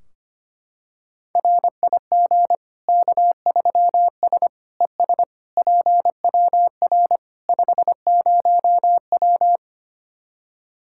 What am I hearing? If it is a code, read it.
Morse code, heard as a keyed tone, about 25 wpm, 700 Hz. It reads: RIG K3S ES PWR 50W
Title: Practice QSO - Exchange 2 - Info - 25wpm